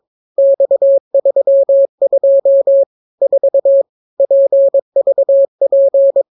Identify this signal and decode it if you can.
Morse code, heard as a keyed tone, about 22 wpm, 550 Hz. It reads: X32 4 PVP